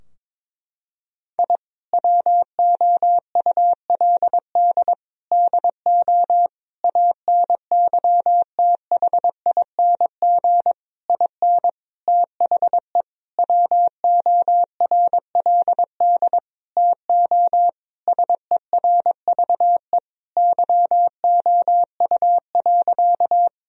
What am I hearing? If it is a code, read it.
Morse code, heard as a keyed tone, about 22 wpm, 700 Hz. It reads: I WOULD DO ANYTHING IN THE WORLD TO SERVE YOU.
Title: Sentences from Top 1000 Words - Review Part 2 - 22wpm